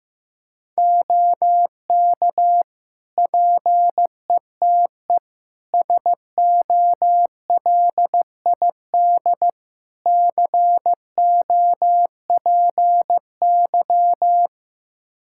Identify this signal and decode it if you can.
Morse code, heard as a keyed tone, about 15 wpm, 700 Hz. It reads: OK PETE SOLID COPY
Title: Practice QSO - Exchange 3 - More Info - 15wpm